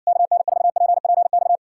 Morse code, heard as a keyed tone, about 66 wpm, 700 Hz. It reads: KN4PPX